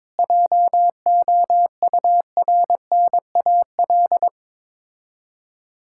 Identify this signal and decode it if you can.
Morse code, heard as a keyed tone, about 22 wpm, 690 Hz. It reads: JOURNAL
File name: Difficult Words with Hints - 22wpm